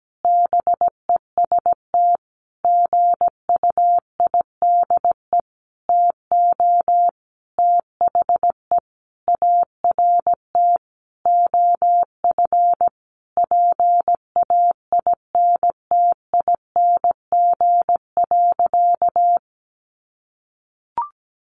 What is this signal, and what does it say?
Morse code, heard as a keyed tone, about 17 wpm, 700 Hz. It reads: BEST GUIDE TO THE ART OF PAINTING.
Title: Sentences from Top 1200 Words 17wpm